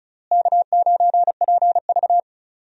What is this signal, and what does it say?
Morse code, heard as a keyed tone, about 35 wpm, 700 Hz. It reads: K9PV
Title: USA Call Signs - 4 Characters - Repeated 2x - 35wpm